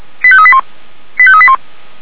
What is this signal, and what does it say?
Five-tone selective call (ZVEI), heard as quick runs of five tones, tones of about 70 ms. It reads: 85381 85381